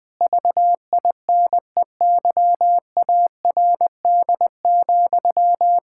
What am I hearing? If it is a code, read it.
Morse code, heard as a keyed tone, about 20 wpm, 700 Hz. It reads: VINEYARD,
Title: Pangram Challenge v2 - 20wpm